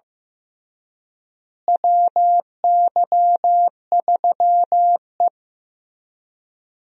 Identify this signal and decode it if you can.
Morse code, heard as a keyed tone, about 15 wpm, 700 Hz. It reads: WY3E
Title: USA Call Signs - 4 Characters - Repeated 3x - 15wpm